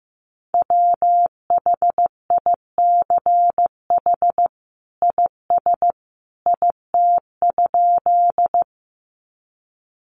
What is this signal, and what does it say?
Morse code, heard as a keyed tone, about 14 wpm, 700 Hz. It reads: WHICH IS IT?
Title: Sentences from Top 100 Words 15wpm